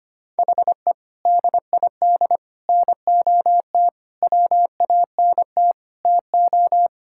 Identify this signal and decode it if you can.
Morse code, heard as a keyed tone, about 25 wpm, 700 Hz. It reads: HE DID NOT WANT TO
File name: practice / 25wpm / Sentences with 4-Word Phrases - 25wpm